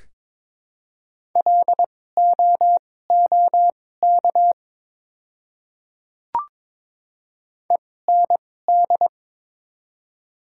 Morse code, heard as a keyed tone, about 22 wpm, spaced as for 16 wpm, 700 Hz. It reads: LOOK END
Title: 2x Character Spacing - Top 200 Words - 22wpm